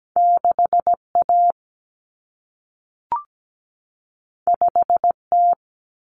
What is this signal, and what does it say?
Morse code, heard as a keyed tone, about 17 wpm, 700 Hz. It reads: 6A 5T
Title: Common Cut Numbers - 2-Digit 17wpm